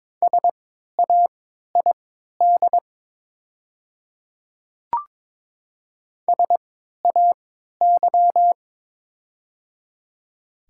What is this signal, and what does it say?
Morse code, heard as a keyed tone, about 22 wpm, spaced as for 12 wpm, 700 Hz. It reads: SAID SAY
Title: 3x Character Spacing - Top 200 Words - 22wpm